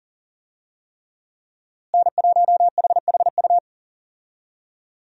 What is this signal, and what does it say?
Morse code, heard as a keyed tone, about 37 wpm, 700 Hz. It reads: N1HHU
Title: USA Call Signs - 5 Characters - Repeated 3x - 40wpm